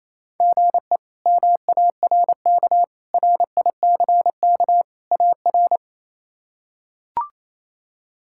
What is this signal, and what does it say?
Morse code, heard as a keyed tone, about 28 wpm, 700 Hz. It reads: GE MARK RICK AR
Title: K1USN SST QSO Practice - 28wpm